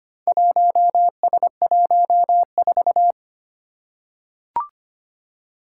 Morse code, heard as a keyed tone, about 25 wpm, 700 Hz. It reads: 1S14
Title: Four Challenging Letter-Number Combinations 25wpm